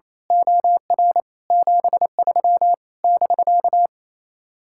Morse code, heard as a keyed tone, about 26 wpm, 700 Hz. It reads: OR 73 <BK>
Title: Speed-Racing Common QSO Phrases Encore - 28wpm